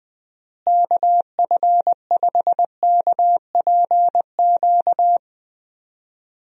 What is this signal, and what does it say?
Morse code, heard as a keyed tone, about 20 wpm, 700 Hz. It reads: KF5KPQ
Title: USA Call Signs - 6 Characters - Repeated 2x - 20wpm